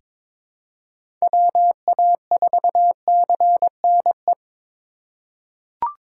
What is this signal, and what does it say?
Morse code, heard as a keyed tone, about 22 wpm, 700 Hz. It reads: WA4CNE
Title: USA Call Signs - 6 Characters - 6x Character Spacing - 22wpm